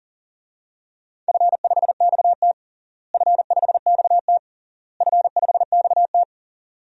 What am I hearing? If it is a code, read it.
Morse code, heard as a keyed tone, about 37 wpm, 700 Hz. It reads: F5XT F5XT F5XT
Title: DXCC-100 Call Signs - 4 Characters - Repeated 3x - 40wpm